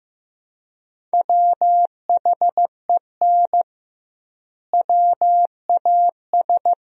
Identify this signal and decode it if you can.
Morse code, heard as a keyed tone, about 16 wpm, 700 Hz. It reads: WHEN WAS
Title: 2x Word-Spacing - Sentences from Top 100 Words - 15wpm